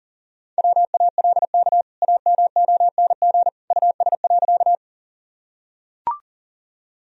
Morse code, heard as a keyed tone, about 40 wpm, 700 Hz. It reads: WALK AMONG US.
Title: Sentences from Top 1000 Words - Review Part 3 - 40wpm